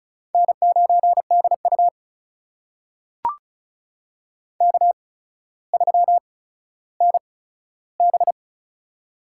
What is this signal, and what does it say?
Morse code, heard as a keyed tone, about 36 wpm, spaced as for 15 wpm, 700 Hz. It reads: N9DU K3NB
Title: USA Call Signs - 4 Characters - 8x Character Spacing - 35wpm